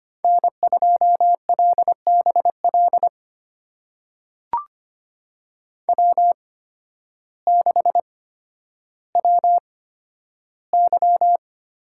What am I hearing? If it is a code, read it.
Morse code, heard as a keyed tone, about 25 wpm, spaced as for 12 wpm, 700 Hz. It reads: N2LBL W6WY
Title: USA Call Signs - 8x Character Spacing - 25wpm